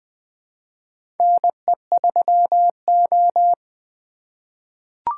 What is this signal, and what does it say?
Morse code, heard as a keyed tone, about 20 wpm, 700 Hz. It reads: NE3O
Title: USA Call Signs - 4 Characters - 4x Character Spacing - 20wpm